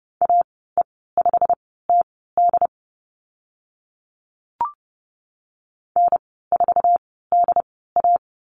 Morse code, heard as a keyed tone, about 30 wpm, spaced as for 17 wpm, 700 Hz. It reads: AE5TD N4DA
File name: Lesson 15g - Number 5 - Call Signs 30wpm